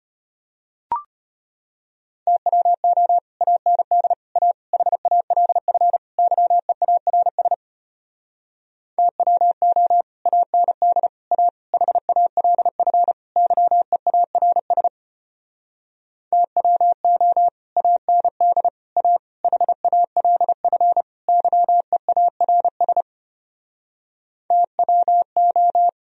Morse code, heard as a keyed tone, about 33 wpm, 700 Hz. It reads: TWO AND A HALF YEARS TWO AND A HALF YEARS TWO AND A HALF YEARS TWO